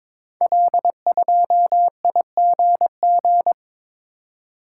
Morse code, heard as a keyed tone, about 22 wpm, 700 Hz. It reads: L2IGG
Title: Code Groups - 5 Letter:Numbers - 22wpm